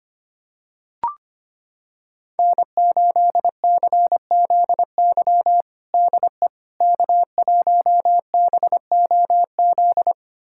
Morse code, heard as a keyed tone, about 25 wpm, 700 Hz. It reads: N8CZY DE K1BOZ
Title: USA Callsign DE Callsign - 5 Characters - 25wpm